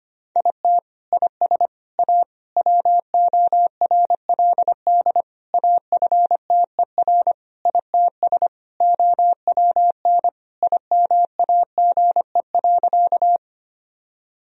Morse code, heard as a keyed tone, about 25 wpm, 700 Hz. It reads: IT IS A WORLD AFTER ITS OWN IMAGE.
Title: Sentences from Top 1000 Words - Review Part 4 - 25wpm